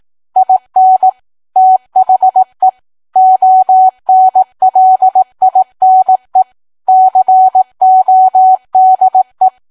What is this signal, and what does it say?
Morse code, heard as a keyed tone, about 18 wpm, 750 Hz. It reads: IN THE ONLINE CODE